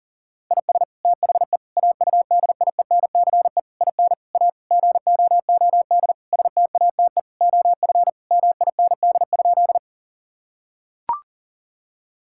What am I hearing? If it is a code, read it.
Morse code, heard as a keyed tone, about 40 wpm, 700 Hz. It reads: IS THE AUDIENCE IN A GOOD STATE OF MIND?
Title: Sentences from Top 1000 Words - Review Part 5 - 40wpm